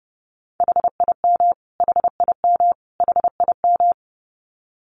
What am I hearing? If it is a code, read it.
Morse code, heard as a keyed tone, about 30 wpm, 700 Hz. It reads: HIM HIM HIM
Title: Top 200 Words Repeated 30wpm